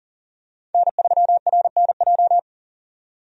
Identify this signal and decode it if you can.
Morse code, heard as a keyed tone, about 40 wpm, 700 Hz. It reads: N3RNJ
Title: USA Call Signs - 5 Characters - 40wpm